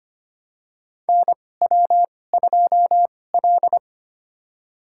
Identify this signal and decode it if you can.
Morse code, heard as a keyed tone, about 25 wpm, spaced as for 18 wpm, 700 Hz. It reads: NW2L
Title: USA Call Signs - 4 Characters - 2x Character Spacing - 25wpm